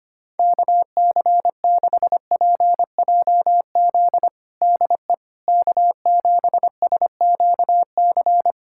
Morse code, heard as a keyed tone, about 25 wpm, 700 Hz. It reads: KC6PJZ DE K7SQC
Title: USA Callsign DE Callsign 25wpm